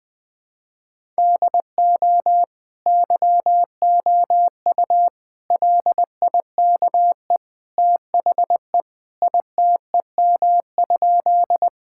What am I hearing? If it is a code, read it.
Morse code, heard as a keyed tone, about 20 wpm, 700 Hz. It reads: DO YOU LIKE THE ITEM?
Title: Sentences from Top 1000 Words - Review Part 6 - 20wpm